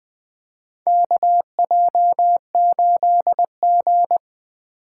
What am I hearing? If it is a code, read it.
Morse code, heard as a keyed tone, about 21 wpm, 700 Hz. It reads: KJ8G
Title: USA Call Signs - 4 Characters - 3x Character Spacing - 20wpm